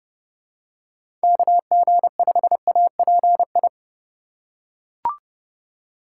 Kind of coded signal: Morse code

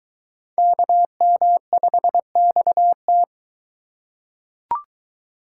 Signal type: Morse code

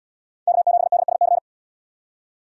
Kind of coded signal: Morse code

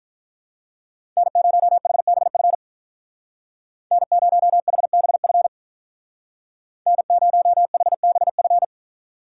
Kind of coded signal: Morse code

